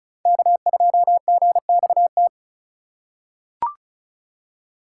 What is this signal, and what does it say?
Morse code, heard as a keyed tone, about 36 wpm, 700 Hz. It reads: K2GXT
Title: USA Call Signs - 8x Character Spacing - 35wpm